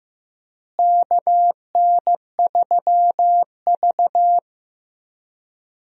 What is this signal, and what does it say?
Morse code, heard as a keyed tone, about 15 wpm, 700 Hz. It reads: KN3V
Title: USA Call Signs - 4 Characters - Repeated 3x - 15wpm